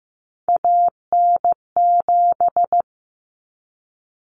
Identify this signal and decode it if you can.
Morse code, heard as a keyed tone, about 15 wpm, 700 Hz. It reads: AN7
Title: Common Cut Numbers - 3-Digit 15wpm